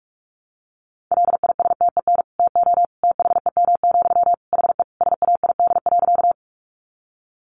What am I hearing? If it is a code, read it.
Morse code, heard as a keyed tone, about 44 wpm, 700 Hz. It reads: LISTEN TO THEM, HE SAID.
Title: Sentences from Top 400 Words 45wpm